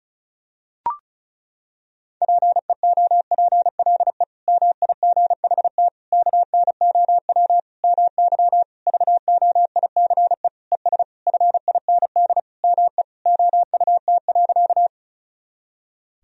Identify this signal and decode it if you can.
Morse code, heard as a keyed tone, about 36 wpm, 700 Hz. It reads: PEOPLE MIGHT KNOW MY VOICE ES FIND ME OUT.
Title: Sentences from Top 500 Words and Abbreviations 35wpm